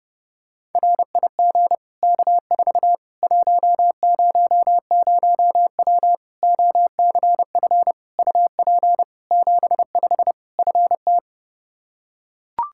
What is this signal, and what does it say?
Morse code, heard as a keyed tone, about 30 wpm, 700 Hz. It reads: RIG K4 100W OCF UP 75 FT
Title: Practice QSO - Exchange 2 - Info - 30wpm